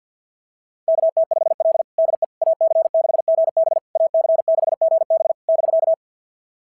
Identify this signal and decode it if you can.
Morse code, heard as a keyed tone, about 46 wpm, 690 Hz. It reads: KT5L DE AC6GB AC6GB <BK>
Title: SOTA QSO Practice 50wpm